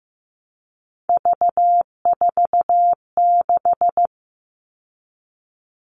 Morse code, heard as a keyed tone, about 15 wpm, 700 Hz. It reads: V46
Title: All Cut Numbers - 3-Digit 15wpm